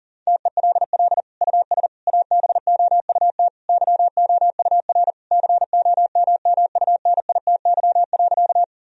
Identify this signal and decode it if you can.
Morse code, heard as a keyed tone, about 37 wpm, 700 Hz. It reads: TELL US ABOUT YOUR COMMUNITY.
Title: Sentences from Top 1000 Words - Review Part 4 - 40wpm